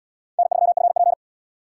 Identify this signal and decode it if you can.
Morse code, heard as a keyed tone, about 68 wpm, 690 Hz. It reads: N3FP